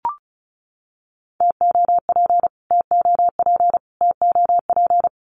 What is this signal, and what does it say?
Morse code, heard as a keyed tone, about 34 wpm, 700 Hz. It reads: TOP TOP TOP
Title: Top 500 Words Repeated 35wpm